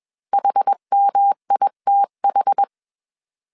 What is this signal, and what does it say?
Morse code, heard as a keyed tone, about 21 wpm, 800 Hz. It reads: HMITH
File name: CW / DET / 1 3 1500C S B 20wpm 1754